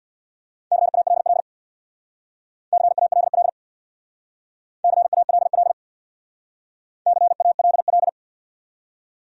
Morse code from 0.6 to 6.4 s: CALL CALL CALL

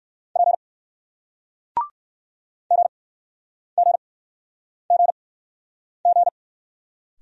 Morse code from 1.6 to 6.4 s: GGGG

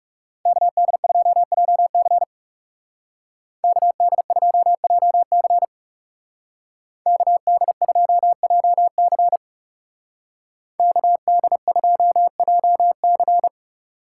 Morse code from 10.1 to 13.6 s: KD2JC

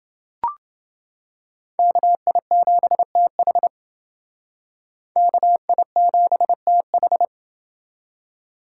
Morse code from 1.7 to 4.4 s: KI7TH

